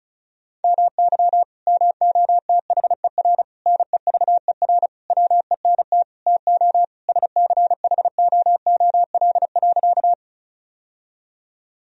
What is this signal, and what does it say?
Morse code, heard as a keyed tone, about 34 wpm, 700 Hz. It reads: MY MOTHER NEVER WENT TO SCHOOL.
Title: Sentences from Top 1000 Words - Review Part 1 - 35wpm